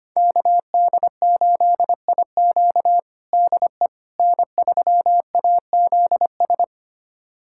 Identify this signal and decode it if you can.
Morse code, heard as a keyed tone, about 25 wpm, 700 Hz. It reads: KD8IQ DE N3AZS
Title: USA Callsign DE Callsign - 5 Characters - 25wpm